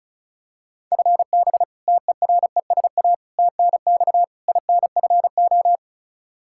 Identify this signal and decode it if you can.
Morse code, heard as a keyed tone, about 36 wpm, 700 Hz. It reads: FB TERESA TNX INFO
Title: Practice QSO - Exchange 3 - More Info - 35wpm